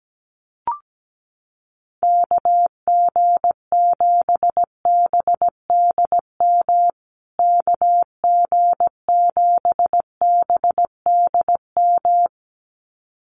Morse code, heard as a keyed tone, about 17 wpm, 700 Hz. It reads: KG7BDM KG7BDM
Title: USA Call Signs Repeated 2x 17wpm